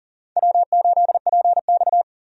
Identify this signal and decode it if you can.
Morse code, heard as a keyed tone, about 40 wpm, 700 Hz. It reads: W8PX
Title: USA Call Signs - 4 Characters - 40wpm